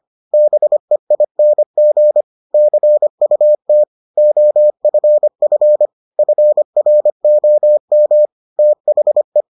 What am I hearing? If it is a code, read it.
Morse code, heard as a keyed tone, about 25 wpm, 600 Hz. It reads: BEING CUT OFF FROM THE